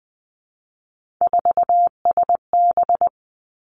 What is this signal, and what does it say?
Morse code, heard as a keyed tone, about 20 wpm, 700 Hz. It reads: 4SB
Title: Three Challenging Letter-Number Combinations 20wpm